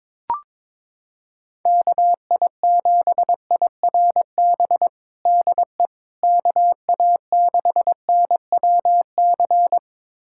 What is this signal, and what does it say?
Morse code, heard as a keyed tone, about 22 wpm, 700 Hz. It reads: KI7IRB DE KA6NWC
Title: USA Callsign DE Callsign - 6 Characters - 22wpm